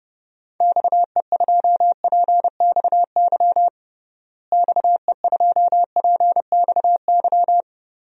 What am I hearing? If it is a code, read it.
Morse code, heard as a keyed tone, about 32 wpm, 700 Hz. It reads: XE2PXY XE2PXY
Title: DXCC-100 Call Signs - 6 Characters - Repeated 3x - 30wpm